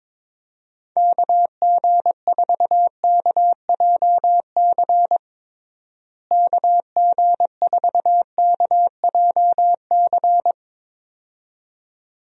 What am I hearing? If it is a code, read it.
Morse code, heard as a keyed tone, about 22 wpm, 700 Hz. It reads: KG4KJC KG4KJC
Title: USA Call Signs - 6 Characters - Repeated 2x - 22wpm